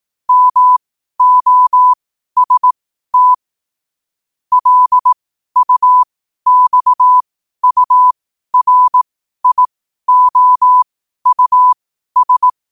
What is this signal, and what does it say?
Morse code, heard as a keyed tone, about 18 wpm, spaced as for 12 wpm, 1000 Hz. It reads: MOST LUXURIOUS
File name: qotdfiles / 12wpm